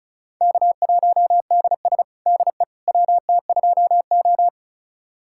Morse code, heard as a keyed tone, about 34 wpm, 700 Hz. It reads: K1DS DE WT2O
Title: USA Callsign DE Callsign - 4 Characters - 35wpm